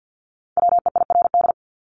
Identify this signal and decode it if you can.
Morse code, heard as a keyed tone, about 50 wpm, 700 Hz. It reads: WEIRD
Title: Top 1251-1500 Words 50wpm